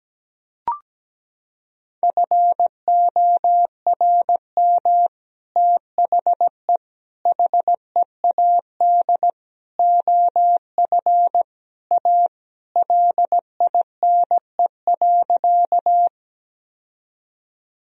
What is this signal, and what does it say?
Morse code, heard as a keyed tone, about 17 wpm, 700 Hz. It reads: FORM THE HEAD OF A LINE.
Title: Sentences from Top 1000 Words - Review Part 1 - 17wpm